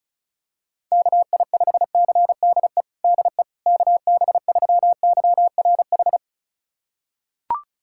Morse code, heard as a keyed tone, about 36 wpm, 700 Hz. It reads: KI5CDE DE KB3YRH
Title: USA Callsign DE Callsign 35wpm